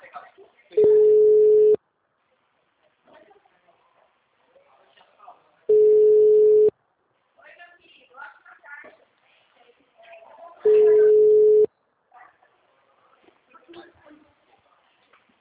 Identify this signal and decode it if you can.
Ringback tone (Europe, 425 Hz), with speech.